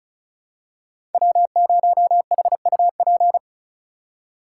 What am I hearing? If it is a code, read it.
Morse code, heard as a keyed tone, about 36 wpm, 690 Hz. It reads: W0HUP
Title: USA Call Signs - 3x Character Spacing - 35wpm